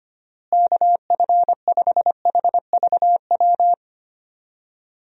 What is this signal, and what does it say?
Morse code, heard as a keyed tone, about 25 wpm, 700 Hz. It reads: KF5HVW
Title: USA Call Signs - 6 Characters - Repeated 3x - 25wpm